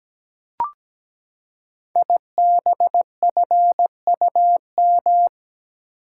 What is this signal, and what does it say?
Morse code, heard as a keyed tone, about 17 wpm, 690 Hz. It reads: IBFUM
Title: Code Groups - 5 Letter:Numbers - 17wpm